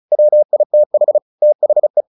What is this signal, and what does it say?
Morse code, heard as a keyed tone, about 34 wpm, 600 Hz. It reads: WITH THE
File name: the old man and the sea 2 35wpm